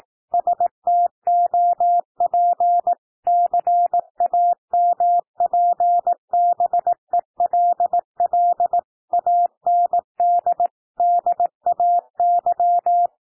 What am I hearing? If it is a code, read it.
Morse code, heard as a keyed tone, about 18 wpm, 700 Hz. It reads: STOP CAMPBELL AND DAY
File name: TEXT PASSAGE 18 wpm